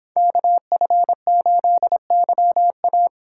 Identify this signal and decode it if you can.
Morse code, heard as a keyed tone, about 26 wpm, 700 Hz. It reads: KF8YA